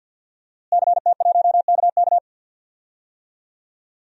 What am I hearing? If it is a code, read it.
Morse code, heard as a keyed tone, about 53 wpm, 700 Hz. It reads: CT1KK
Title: DXCC-100 Call Signs - 5 Characters - Repeated 3x - 50wpm